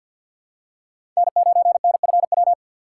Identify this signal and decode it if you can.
Morse code, heard as a keyed tone, about 47 wpm, 700 Hz. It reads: N9NFW
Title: USA Call Signs - 5 Characters - 6x Character Spacing - 50wpm